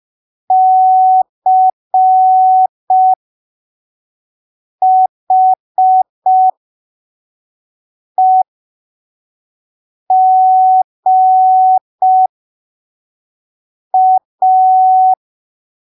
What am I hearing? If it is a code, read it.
Morse code, heard as a keyed tone, about 5 wpm, spaced as for 3 wpm, 750 Hz. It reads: CHEGA
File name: arquivos / texto / texto 13 05ppm